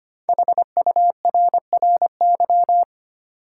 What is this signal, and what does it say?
Morse code, heard as a keyed tone, about 25 wpm, 700 Hz. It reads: HURRY
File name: 5 Letter Words - First 3 Letters Given - 25wpm